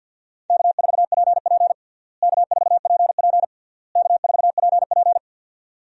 Morse code, heard as a keyed tone, about 46 wpm, 700 Hz. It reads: K4PP K4PP K4PP
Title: USA Call Signs - 4 Characters - Repeated 3x - 50wpm